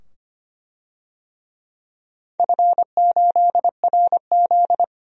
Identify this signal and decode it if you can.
Morse code, heard as a keyed tone, about 25 wpm, 700 Hz. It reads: F8RZ
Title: DXCC-100 Call Signs - 4 Characters - 25wpm